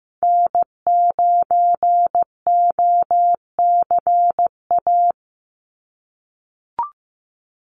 Morse code, heard as a keyed tone, about 15 wpm, 700 Hz. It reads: N9OCA
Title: USA Call Signs 15wpm